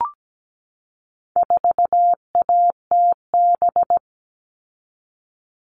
Morse code, heard as a keyed tone, about 17 wpm, 700 Hz. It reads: 4ATB
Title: All Cut Numbers - 4-Digit 17wpm